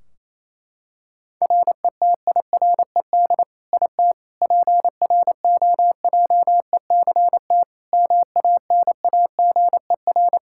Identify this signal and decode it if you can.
Morse code, heard as a keyed tone, about 28 wpm, 700 Hz. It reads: RETIRED IT PROJECT MANAGER
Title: Practice QSO - Exchange 3 - More Info - 28wpm